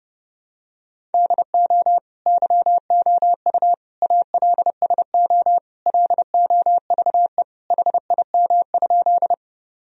Morse code, heard as a keyed tone, about 30 wpm, 700 Hz. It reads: DO YOU ALSO LOVE HIM?
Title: Sentences from Top 1000 Words - Review Part 1 - 30wpm